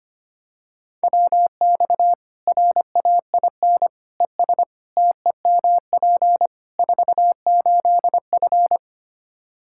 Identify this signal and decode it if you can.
Morse code, heard as a keyed tone, about 25 wpm, 700 Hz. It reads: WX RAIN ES TEMP 48F
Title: Practice QSO - Exchange 2 - Info - 25wpm